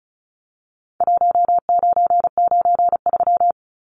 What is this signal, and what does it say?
Morse code, heard as a keyed tone, about 34 wpm, 700 Hz. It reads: 1993
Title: Numbers - Contest Style - 4-Digit 35wpm